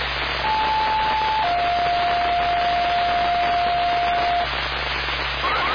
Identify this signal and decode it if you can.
Two-tone page, heard as a one-second tone, then a three-second tone, about 870 then 700 Hz, with noise. It is heard from 0.4 to 4.4 s.